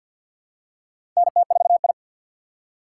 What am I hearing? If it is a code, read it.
Morse code, heard as a keyed tone, about 54 wpm, 700 Hz. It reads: NT4I
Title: USA Call Signs - 4 Characters - 6x Character Spacing - 50wpm